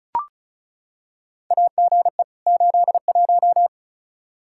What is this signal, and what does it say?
Morse code, heard as a keyed tone, about 36 wpm, 700 Hz. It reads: AGE 81
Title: Practice QSO - Exchange 3 - More Info - 35wpm